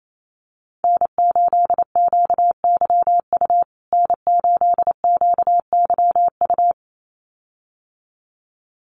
Morse code, heard as a keyed tone, about 28 wpm, 700 Hz. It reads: N8QYU N8QYU
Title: USA Call Signs Repeated 2x 28wpm